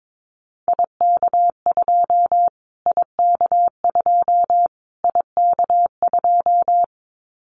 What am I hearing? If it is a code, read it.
Morse code, heard as a keyed tone, about 22 wpm, 700 Hz. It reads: IK2 IK2 IK2
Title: Callsign Prefixes for Top 25 DXCC - Repeated 3x - 22wpm